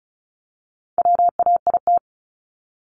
Morse code, heard as a keyed tone, about 34 wpm, 700 Hz. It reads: WAIT
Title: ING Suffix - Top 100 Words with Base Words 35wpm